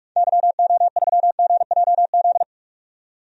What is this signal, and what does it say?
Morse code, heard as a keyed tone, about 42 wpm, 700 Hz. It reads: YO3GJZ
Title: DXCC-100 Call Signs - 6 Characters - 45wpm